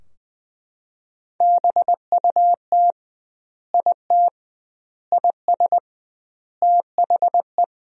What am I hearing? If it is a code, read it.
Morse code, heard as a keyed tone, about 20 wpm, 700 Hz. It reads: BUT IT IS THE
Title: 2x Word-Spacing - Sentences from Top 300 Words - 20wpm